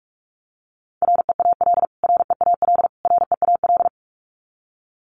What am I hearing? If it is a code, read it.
Morse code, heard as a keyed tone, about 44 wpm, 700 Hz. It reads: REAL REAL REAL
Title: Top 500 Words Repeated 45wpm